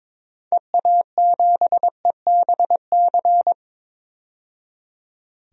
Morse code, heard as a keyed tone, about 22 wpm, 700 Hz. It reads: EA7EBC
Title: DXCC-100 Call Signs - 6 Characters - 22wpm